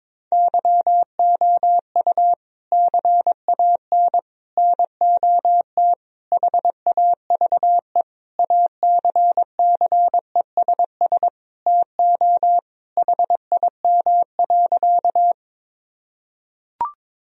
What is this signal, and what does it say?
Morse code, heard as a keyed tone, about 22 wpm, 700 Hz. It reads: YOU CAN NOT HAVE ACCESS TO HIM.
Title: Sentences from Top 1000 Words - Review Part 4 - 22wpm